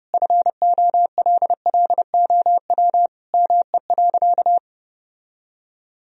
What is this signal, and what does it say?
Morse code, heard as a keyed tone, about 30 wpm, 700 Hz. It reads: FOLLOW ME.
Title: Sentences from Top 1000 Words - Review Part 2 - 30wpm